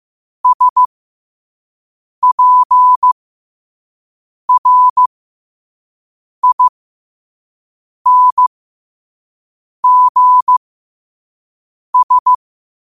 Morse code, heard as a keyed tone, about 15 wpm, spaced as for 5 wpm, 1000 Hz. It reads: SPRINGS